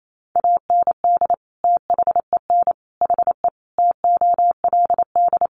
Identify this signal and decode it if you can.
Morse code, heard as a keyed tone, about 28 wpm, 700 Hz. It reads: AND THEN HE TOLD
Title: Sentences from Top 400 Words 28wpm